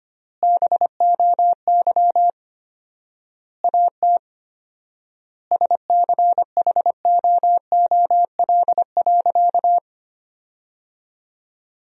Morse code, heard as a keyed tone, about 25 wpm, 700 Hz. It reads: BOY AT SCHOOL.
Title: 4x Word-Spacing - Sentences from Top 300 Words - 25wpm